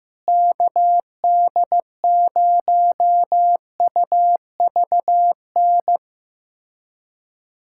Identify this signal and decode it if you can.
Morse code, heard as a keyed tone, about 15 wpm, 700 Hz. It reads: KD0UVN
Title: USA Callsign DE Callsign 15wpm